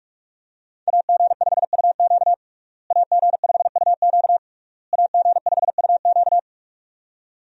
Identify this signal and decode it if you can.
Morse code, heard as a keyed tone, about 42 wpm, 700 Hz. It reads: AG5UQ AG5UQ AG5UQ
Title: USA Call Signs - 5 Characters - Repeated 3x - 45wpm